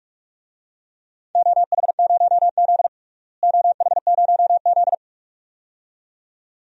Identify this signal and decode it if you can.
Morse code, heard as a keyed tone, about 46 wpm, 700 Hz. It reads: OH0Z OH0Z
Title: DXCC-100 Call Signs - 4 Characters - Repeated 2x - 45wpm